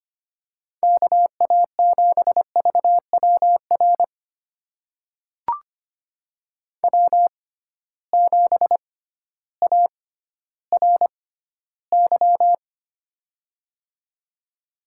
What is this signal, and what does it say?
Morse code, heard as a keyed tone, about 25 wpm, spaced as for 13 wpm, 700 Hz. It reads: KA7VWR W7ARY
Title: USA Call Signs - 6x Character Spacing - 25wpm